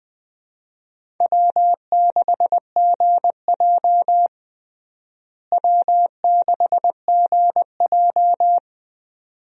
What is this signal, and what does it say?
Morse code, heard as a keyed tone, about 20 wpm, 700 Hz. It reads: W6GJ W6GJ
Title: USA Call Signs - 4 Characters - Repeated 2x - 20wpm